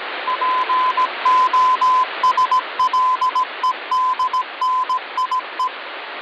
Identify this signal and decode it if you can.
Morse code (Russian alphabet), heard as a keyed tone, about 18 wpm, 1020 Hz, with radio noise. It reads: ПОСЛЕДНИЕ
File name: original